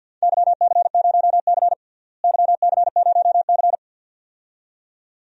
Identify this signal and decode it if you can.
Morse code, heard as a keyed tone, about 51 wpm, 700 Hz. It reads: YC0C YC0C
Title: DXCC-100 Call Signs - 4 Characters - Repeated 2x - 50wpm